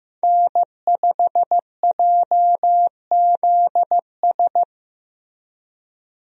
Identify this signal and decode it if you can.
Morse code, heard as a keyed tone, about 15 wpm, 700 Hz. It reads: N5JZS
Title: USA Call Signs - 5 Characters - Repeated 2x - 15wpm